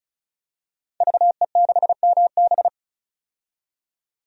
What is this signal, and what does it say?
Morse code, heard as a keyed tone, about 36 wpm, 700 Hz. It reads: VE6MB
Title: DXCC-100 Call Signs - 5 Characters - 35wpm